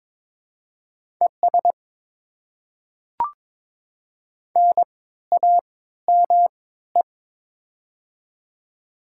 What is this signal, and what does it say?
Morse code, heard as a keyed tone, about 22 wpm, spaced as for 14 wpm, 700 Hz. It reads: ES NAME
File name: Top 20 QSO Elements - 3x Character Spacing - 22wpm